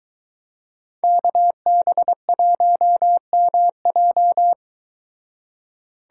Morse code, heard as a keyed tone, about 23 wpm, 700 Hz. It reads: KB1MJ